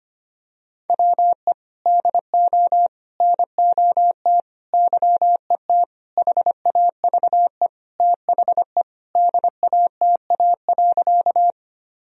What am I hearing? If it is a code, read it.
Morse code, heard as a keyed tone, about 25 wpm, 700 Hz. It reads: WE DO NOT YET HAVE THE DATA.
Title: Sentences from Top 1000 Words - Review Part 3 - 25wpm